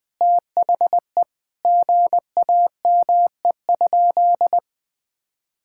Morse code, heard as a keyed tone, about 20 wpm, 700 Hz. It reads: THE GAME?
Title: Sentences from Top 1000 Words - Review Part 6 - 20wpm